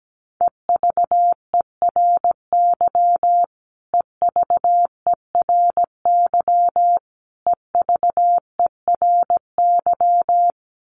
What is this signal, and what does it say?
Morse code, heard as a keyed tone, about 17 wpm, 700 Hz. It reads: EVERY EVERY EVERY
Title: Top 200 Words Repeated 17wpm